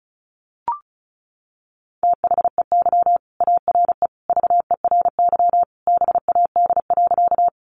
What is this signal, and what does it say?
Morse code, heard as a keyed tone, about 34 wpm, 700 Hz. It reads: THEY ARE VERY BAD.
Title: Sentences from Top 700 Words 35wpm